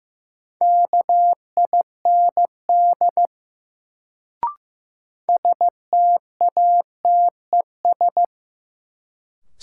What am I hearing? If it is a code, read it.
Morse code, heard as a keyed tone, about 15 wpm, 700 Hz. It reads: KIND STATES
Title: Sets of 2 Words - New Word Drill - 15wpm